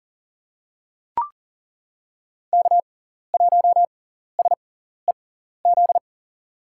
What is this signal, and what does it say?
Morse code, heard as a keyed tone, about 37 wpm, spaced as for 14 wpm, 700 Hz. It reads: K1SEZ
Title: USA Call Signs - 5 Characters - 6x Character Spacing - 40wpm